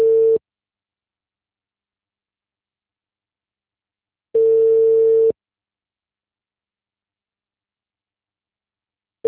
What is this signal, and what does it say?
Ringback tone (China, 450 Hz).